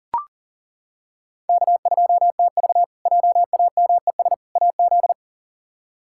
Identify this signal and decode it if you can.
Morse code, heard as a keyed tone, about 40 wpm, 700 Hz. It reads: K2TV JAMES AZ
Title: K1USN SST QSO Practice - 40wpm